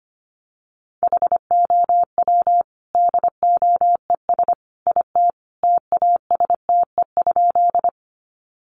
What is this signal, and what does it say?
Morse code, heard as a keyed tone, about 25 wpm, 690 Hz. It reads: HOW DOES IT TASTE?
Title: Sentences from Top 1100 Words 25wpm